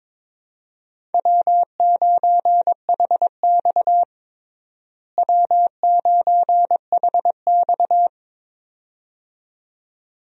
Morse code, heard as a keyed tone, about 22 wpm, 700 Hz. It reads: W9HX W9HX
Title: USA Call Signs - 4 Characters - Repeated 2x - 22wpm